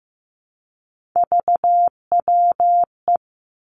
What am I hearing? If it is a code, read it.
Morse code, heard as a keyed tone, about 15 wpm, 700 Hz. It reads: VWE
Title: Three Challenging Letter-Number Combinations 15wpm